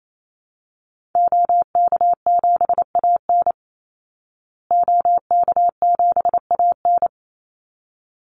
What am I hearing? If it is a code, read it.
Morse code, heard as a keyed tone, about 28 wpm, 700 Hz. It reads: OK7AN OK7AN
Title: Call Signs Repeated 3x - 28wpm